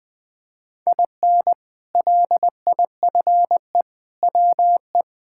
Morse code, heard as a keyed tone, about 20 wpm, 700 Hz. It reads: IN LIFE WE
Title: Sentences with 7-Word Phrases - 20wpm